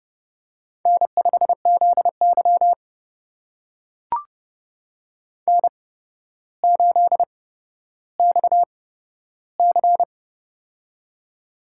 Morse code, heard as a keyed tone, about 30 wpm, spaced as for 13 wpm, 700 Hz. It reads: N5ZY N8XC
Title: USA Call Signs - 4 Characters - 8x Character Spacing - 30wpm